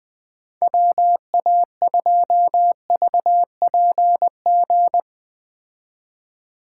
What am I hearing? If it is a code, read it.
Morse code, heard as a keyed tone, about 20 wpm, 700 Hz. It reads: WA2VPG
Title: Speed Racing - USA 2x3 Call Signs 17wpm